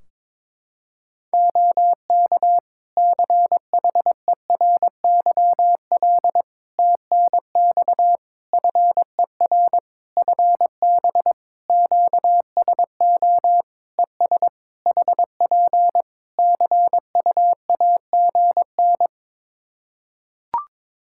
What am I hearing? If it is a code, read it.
Morse code, heard as a keyed tone, about 22 wpm, 700 Hz. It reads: OK CHERYL TNX FER FB QSO ES HP CUAGN
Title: Practice QSO - Exchange 4 - Ending - 22wpm